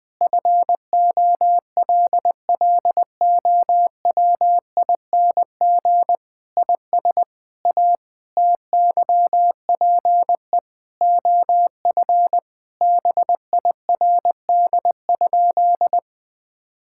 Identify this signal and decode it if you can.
Morse code, heard as a keyed tone, about 20 wpm, 700 Hz. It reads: FOLLOWING IS A TYPE OF BIRD?
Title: Sentences with 8-Word Phrases - 20wpm